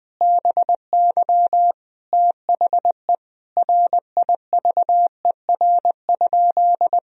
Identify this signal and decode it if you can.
Morse code, heard as a keyed tone, about 20 wpm, 700 Hz. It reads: BY THE RIVER?
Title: Sentences from Top 1000 Words - Review Part 1 - 20wpm